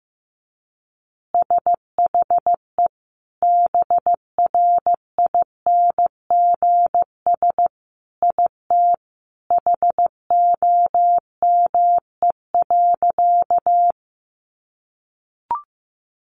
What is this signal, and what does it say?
Morse code, heard as a keyed tone, about 15 wpm, 700 Hz. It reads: SHE BRINGS IT HOME.
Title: Easy Phrases 15wpm